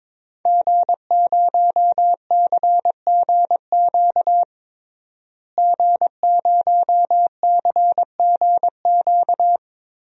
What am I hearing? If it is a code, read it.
Morse code, heard as a keyed tone, about 22 wpm, 700 Hz. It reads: G0CGQ G0CGQ
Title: DXCC-100 Call Signs - 5 Characters - Repeated 3x - 22wpm